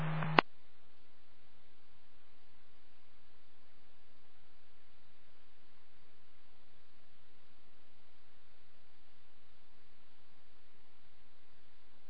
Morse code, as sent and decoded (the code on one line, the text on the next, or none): none